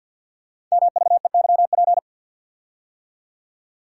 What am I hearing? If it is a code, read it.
Morse code, heard as a keyed tone, about 50 wpm, 700 Hz. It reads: MVEYP
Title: Code Groups - 5 Letters - 50wpm